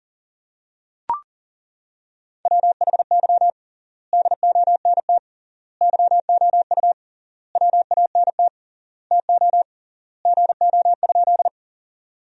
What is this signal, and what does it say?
Morse code, heard as a keyed tone, about 40 wpm, 700 Hz. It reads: WHY DONT YOU WANT TO GO?
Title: 3x Word-Spacing - Sentences from Top 300 Words - 40wpm